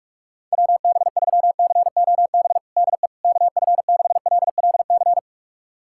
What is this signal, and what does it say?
Morse code, heard as a keyed tone, about 46 wpm, 700 Hz. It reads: WB3COB DE KF6LLC
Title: USA Callsign DE Callsign - 6 Characters - 45wpm